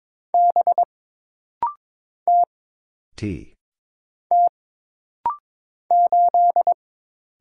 Morse code, heard as a keyed tone, about 22 wpm, 700 Hz. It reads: B T T 8
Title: Single Letter-Number Alt - Rapid-Fire 22wpm